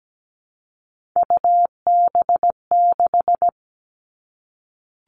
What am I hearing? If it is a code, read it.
Morse code, heard as a keyed tone, about 17 wpm, 700 Hz. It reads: UB6
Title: All Cut Numbers - 3-Digit 17wpm